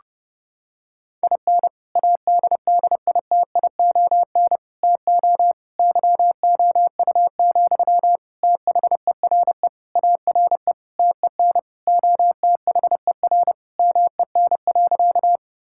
Morse code, heard as a keyed tone, about 32 wpm, 700 Hz. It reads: IN ADDITION TO YOU, THERE ARE TEN OTHER MEN.
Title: Sentences from Top 1000 Words - Review Part 6 - 30wpm